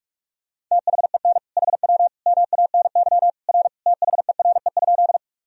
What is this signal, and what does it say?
Morse code, heard as a keyed tone, about 46 wpm, 700 Hz. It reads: THEN HW MANY R THERE?
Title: Sentences from Top 100 Words and Abbreviations 45wpm